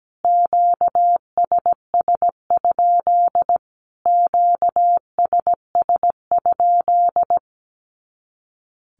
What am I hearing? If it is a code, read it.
Morse code, heard as a keyed tone, about 17 wpm, 700 Hz. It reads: QSS? QSS?
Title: Q-Codes Repeated 3x 17wpm